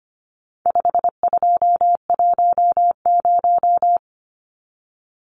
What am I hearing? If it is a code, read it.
Morse code, heard as a keyed tone, about 25 wpm, 700 Hz. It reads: 5210
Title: Numbers - 4-Digit 25wpm